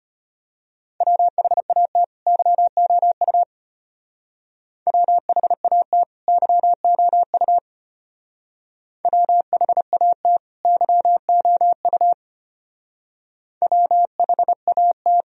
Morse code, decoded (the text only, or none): WHAT YOU WHAT YOU WHAT YOU WHAT